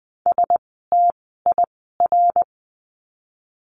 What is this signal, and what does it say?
Morse code, heard as a keyed tone, about 20 wpm, spaced as for 15 wpm, 700 Hz. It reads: STIR
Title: 068 Lesson 12f - Letter d - Words 20wpm